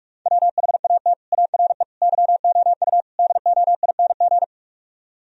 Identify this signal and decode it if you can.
Morse code, heard as a keyed tone, about 44 wpm, 700 Hz. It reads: WHAT ARE YOU DOING